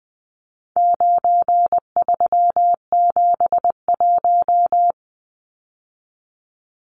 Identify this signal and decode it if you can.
Morse code, heard as a keyed tone, about 19 wpm, 700 Hz. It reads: 9371
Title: Numbers - 4-Digit 20wpm